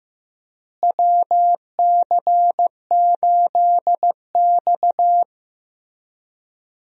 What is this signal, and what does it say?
Morse code, heard as a keyed tone, about 15 wpm, 700 Hz. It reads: WC8X
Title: USA Call Signs - 4 Characters - Repeated 3x - 15wpm